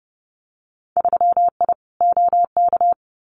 Morse code, heard as a keyed tone, about 30 wpm, 700 Hz. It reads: 3I OK
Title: Winter Field Day Exchanges Encore 30wpm